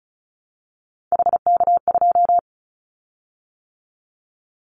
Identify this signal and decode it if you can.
Morse code, heard as a keyed tone, about 34 wpm, 700 Hz. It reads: HK2
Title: Callsign Prefixes for Top 100 DXCC - 35wpm